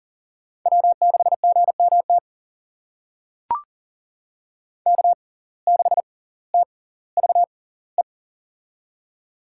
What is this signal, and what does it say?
Morse code, heard as a keyed tone, about 39 wpm, spaced as for 20 wpm, 700 Hz. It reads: W6GMT K6TVE
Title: USA Call Signs - 5 Characters - 6x Character Spacing - 40wpm